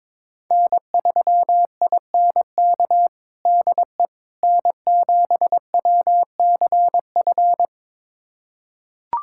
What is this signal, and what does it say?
Morse code, heard as a keyed tone, about 22 wpm, 700 Hz. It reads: N3INK DE N7WCF
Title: USA Callsign DE Callsign - 5 Characters - 22wpm